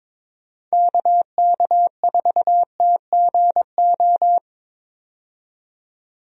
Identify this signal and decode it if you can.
Morse code, heard as a keyed tone, about 22 wpm, 700 Hz. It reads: KK4TGO
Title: USA Call Signs - 6 Characters - 22wpm